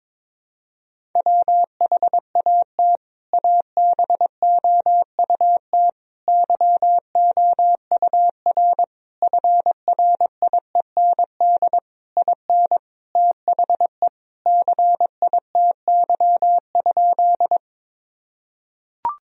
Morse code, heard as a keyed tone, about 22 wpm, 700 Hz. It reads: WHAT ABOUT YOUR FRIEND IN THE CITY?
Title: Sentences from Top 1000 Words - Review Part 1 - 22wpm